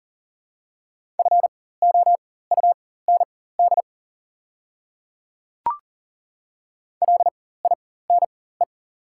Morse code, heard as a keyed tone, about 40 wpm, spaced as for 19 wpm, 700 Hz. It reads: FOUND LINE
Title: 4x Character Spacing - Top 300 Words - 40wpm